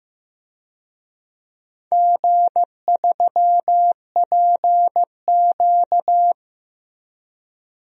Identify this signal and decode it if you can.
Morse code, heard as a keyed tone, about 15 wpm, 700 Hz. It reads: G3PQ
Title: DXCC-100 Call Signs - 4 Characters - Repeated 3x - 15wpm